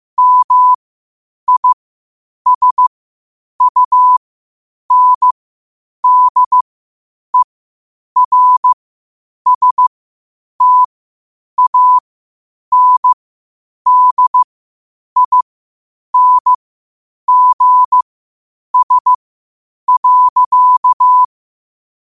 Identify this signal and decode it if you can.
Morse code, heard as a keyed tone, about 15 wpm, spaced as for 8 wpm, 1000 Hz. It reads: MISUNDERSTANDINGS.
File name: qotdfiles / 8wpm